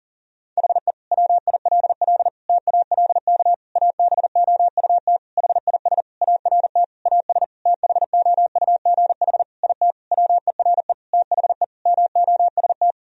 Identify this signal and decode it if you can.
Morse code, heard as a keyed tone, about 40 wpm, 700 Hz. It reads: HE WILL TALK ABOUT HIS ART AS THOUGH IT WERE THE MOST
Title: Sentences from Top 1000 Words - Review Part 3 - 40wpm